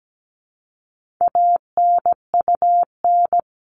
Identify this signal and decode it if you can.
Morse code, heard as a keyed tone, about 17 wpm, 700 Hz. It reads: ANUN
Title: All Cut Numbers - 4-Digit 17wpm